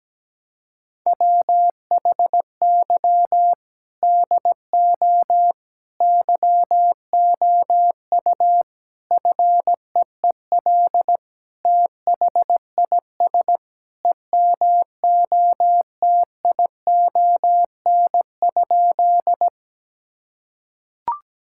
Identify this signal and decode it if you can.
Morse code, heard as a keyed tone, about 17 wpm, 700 Hz. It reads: WHY DO YOU FEEL THIS EMOTION?
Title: Sentences from Top 1600 Words 17wpm